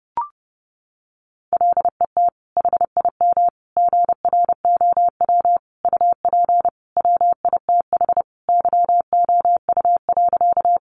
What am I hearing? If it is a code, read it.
Morse code, heard as a keyed tone, about 30 wpm, 700 Hz. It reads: LET HIM GROW UP WITH YOU.
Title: Sentences from Top 300 Words 30wpm